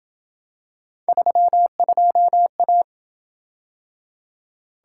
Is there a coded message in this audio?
Yes: Morse code from 1.1 to 2.8 s, heard as a keyed tone, about 26 wpm, 700 Hz. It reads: 32A